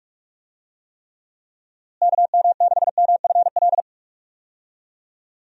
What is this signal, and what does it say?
Morse code, heard as a keyed tone, about 41 wpm, 700 Hz. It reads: KM6MFL
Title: USA Call Signs - 6 Characters - Repeated 2x - 45wpm